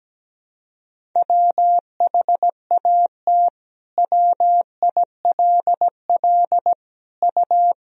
Morse code, heard as a keyed tone, about 17 wpm, 700 Hz. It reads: WHAT WILL U
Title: Sentences from Top 200 Words and Abbreviations 17wpm